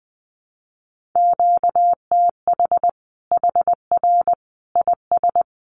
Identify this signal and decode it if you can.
Morse code, heard as a keyed tone, about 20 wpm, 700 Hz. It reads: QTH HR IS
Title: Common QSO Phrases 20wpm